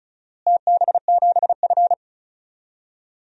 Morse code, heard as a keyed tone, about 34 wpm, 700 Hz. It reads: TB7F
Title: DXCC-100 Call Signs - 4 Characters - Repeated 3x - 35wpm